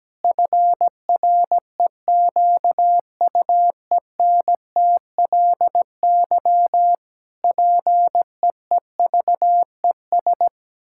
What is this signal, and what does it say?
Morse code, heard as a keyed tone, about 17 wpm, 700 Hz. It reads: FREQUENTLY PEEVES
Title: Pangram Challenge v2 - 17wpm